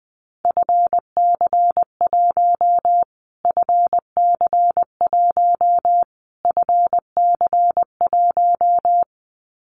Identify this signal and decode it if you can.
Morse code, heard as a keyed tone, about 20 wpm, 700 Hz. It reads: FC1 FC1 FC1
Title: Callsign Prefixes for Top 25 DXCC - Repeated 3x - 20wpm